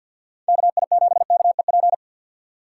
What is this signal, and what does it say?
Morse code, heard as a keyed tone, about 52 wpm, 700 Hz. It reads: KI7KEP
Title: USA Call Signs - 6 Characters - 3x Character Spacing - 50wpm